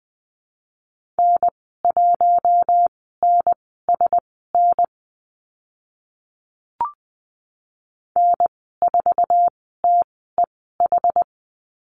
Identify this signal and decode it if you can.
Morse code, heard as a keyed tone, about 20 wpm, spaced as for 14 wpm, 700 Hz. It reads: N1NSN N4TEH
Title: 076 Lesson 13g - Letter L - Call Signs 20wpm